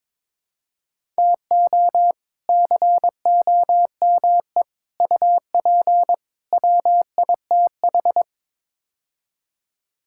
Morse code, heard as a keyed tone, about 22 wpm, 700 Hz. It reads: TO COME UP WITH
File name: practice / 22wpm / Build A Phrase - 5-Word Phrases - 22wpm